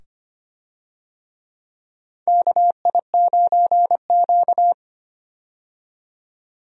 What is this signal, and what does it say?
Morse code, heard as a keyed tone, about 25 wpm, 700 Hz. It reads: KI9Q
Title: USA Call Signs - 4 Characters - 25wpm